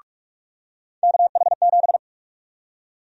Morse code, heard as a keyed tone, about 44 wpm, 700 Hz. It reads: KH7
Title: USA Call Sign Prefixes - 45wpm